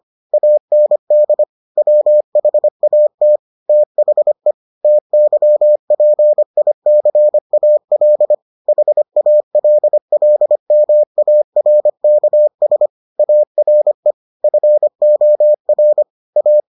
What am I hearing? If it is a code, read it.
Morse code, heard as a keyed tone, about 25 wpm, 600 Hz. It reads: AND WHAT THE TYPICAL HALLMARKS ARE FOR A